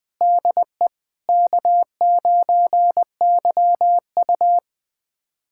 Morse code, heard as a keyed tone, about 21 wpm, 700 Hz. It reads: DE K9YU
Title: USA Callsign DE Callsign - 4 Characters - 20wpm